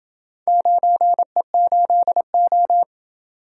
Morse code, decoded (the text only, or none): 9E8O